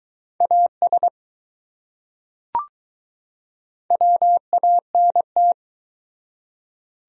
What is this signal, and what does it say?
Morse code, heard as a keyed tone, about 20 wpm, 700 Hz. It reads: AS WANT